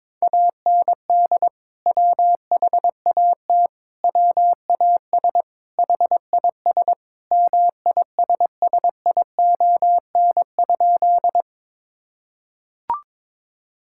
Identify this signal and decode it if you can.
Morse code, heard as a keyed tone, about 22 wpm, 700 Hz. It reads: AND WHAT WAS HIS MISSION?
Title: Sentences from Top 1600 Words 22wpm